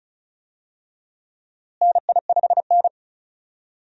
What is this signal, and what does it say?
Morse code, heard as a keyed tone, about 36 wpm, 700 Hz. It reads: NI5N
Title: USA Call Signs - 4 Characters - Repeated 2x - 35wpm